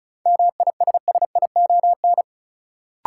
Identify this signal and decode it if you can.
Morse code, heard as a keyed tone, about 36 wpm, 700 Hz. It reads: MISSION
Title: Sentences from Top 1600 Words - New Word Drill - 35wpm